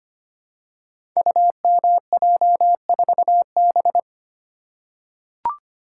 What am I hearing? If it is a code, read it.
Morse code, heard as a keyed tone, about 25 wpm, 700 Hz. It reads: UMJ4B
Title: Code Groups - 5 Letter:Numbers - 25wpm